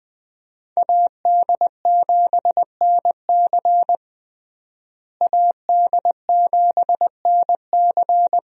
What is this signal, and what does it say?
Morse code, heard as a keyed tone, about 21 wpm, 700 Hz. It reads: AD7NC AD7NC
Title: USA Call Signs - 5 Characters - Repeated 3x - 20wpm